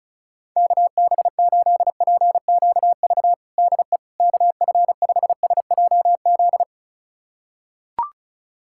Morse code, heard as a keyed tone, about 36 wpm, 700 Hz. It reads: KB8PQV DE KF5SJZ
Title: USA Callsign DE Callsign - 6 Characters - 35wpm